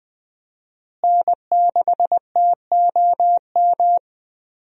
Morse code, heard as a keyed tone, about 20 wpm, 700 Hz. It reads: N6TOM
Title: USA Call Signs - 5 Characters - Repeated 2x - 20wpm